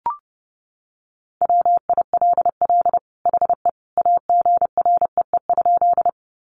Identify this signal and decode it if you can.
Morse code, heard as a keyed tone, about 30 wpm, 700 Hz. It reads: WILL HE AGREE?
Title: Sentences from Top 1100 Words 30wpm